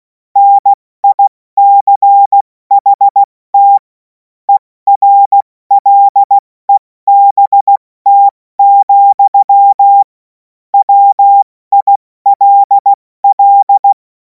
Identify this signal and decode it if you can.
Morse code, heard as a keyed tone, about 16 wpm, 800 Hz. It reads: NICHT ERLEBT, WILL
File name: Morsekurs / Witze / Witz 11. August 2026 70 F80